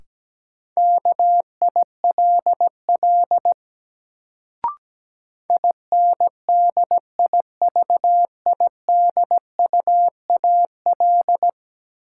Morse code, heard as a keyed tone, about 17 wpm, 700 Hz. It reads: KILL INDIVIDUAL
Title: Sentences from Top 1400 Words - New Word Drill Alt - 17wpm